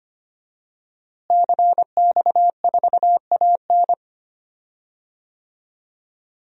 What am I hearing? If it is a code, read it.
Morse code, heard as a keyed tone, about 25 wpm, 700 Hz. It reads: CX4AN
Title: DXCC-100 Call Signs - 5 Characters - Repeated 2x - 25wpm